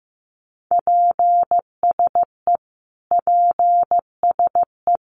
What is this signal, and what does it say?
Morse code, heard as a keyed tone, about 15 wpm, 700 Hz. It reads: PSE PSE
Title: Abbreviations Repeated 3x - 15wpm